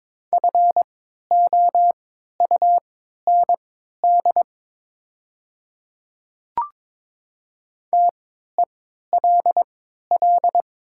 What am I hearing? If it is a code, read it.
Morse code, heard as a keyed tone, about 22 wpm, spaced as for 12 wpm, 700 Hz. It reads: FOUND TELL
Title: 3x Character Spacing - Top 300 Words - 22wpm